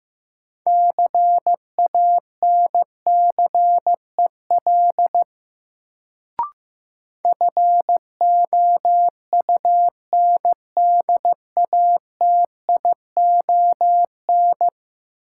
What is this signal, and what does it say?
Morse code, heard as a keyed tone, about 15 wpm, 700 Hz. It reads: CANCEL FOUNDATION
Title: Sentences from Top 1200 Words - New Word Drill - 15wpm